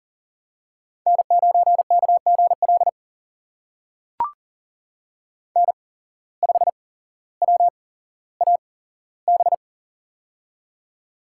Morse code, heard as a keyed tone, about 39 wpm, spaced as for 17 wpm, 700 Hz. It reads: N9KGL N5WAB
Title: USA Call Signs - 5 Characters - 8x Character Spacing - 40wpm